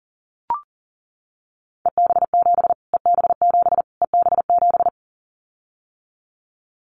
Morse code, heard as a keyed tone, about 40 wpm, 700 Hz. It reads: EB7 EB7 EB7
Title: Callsign Prefixes for Top 25 DXCC - Repeated 3x - 40wpm